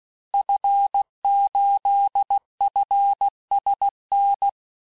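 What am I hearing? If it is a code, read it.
Morse code, heard as a keyed tone, about 16 wpm, 800 Hz. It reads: F8FSN
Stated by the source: Morse course text, end of QSO, sent at 15 words per minute.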